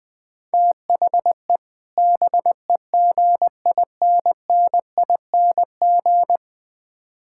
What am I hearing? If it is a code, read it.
Morse code, heard as a keyed tone, about 20 wpm, 700 Hz. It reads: THE BEGINNING
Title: Build A Phrase - 3-Word Phrases - 20wpm